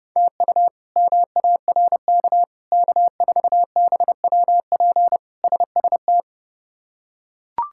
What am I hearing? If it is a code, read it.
Morse code, heard as a keyed tone, about 30 wpm, 700 Hz. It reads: TU MARK K4BWP SST
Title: K1USN SST QSO Practice - 30wpm